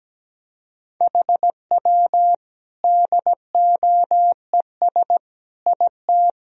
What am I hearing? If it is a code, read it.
Morse code, heard as a keyed tone, about 17 wpm, 700 Hz. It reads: HW DOES IT
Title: Sentences from Top 400 Words and Abbreviations 17wpm